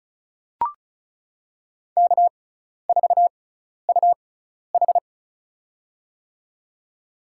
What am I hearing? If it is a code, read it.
Morse code, heard as a keyed tone, about 34 wpm, spaced as for 12 wpm, 700 Hz. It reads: K4UH
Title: USA Call Signs - 4 Characters - 6x Character Spacing - 35wpm